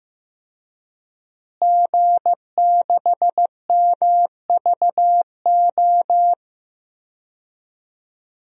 Morse code, heard as a keyed tone, about 15 wpm, 700 Hz. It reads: G6MVO
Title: DXCC-100 Call Signs - 5 Characters - Repeated 2x - 15wpm